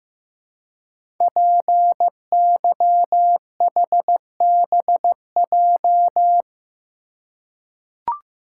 Morse code, heard as a keyed tone, about 15 wpm, 700 Hz. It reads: PYHBJ
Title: Code Groups - 5 Letters - 15wpm